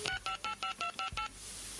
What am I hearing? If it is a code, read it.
DTMF, heard as telephone touch tones, tones of about 100 ms. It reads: 6585258